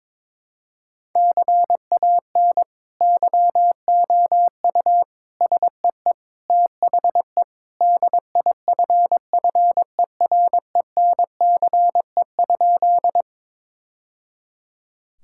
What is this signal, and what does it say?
Morse code, heard as a keyed tone, about 22 wpm, 700 Hz. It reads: CAN YOU SEE THE DIFFERENCE?
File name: practice / 22wpm / Sentences with 2-Word Phrases - 22wpm